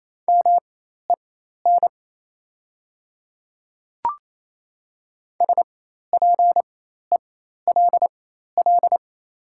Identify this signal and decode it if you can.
Morse code, heard as a keyed tone, about 27 wpm, spaced as for 13 wpm, 700 Hz. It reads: MEN SPELL
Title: 4x Character Spacing - Top 200 Words - 28wpm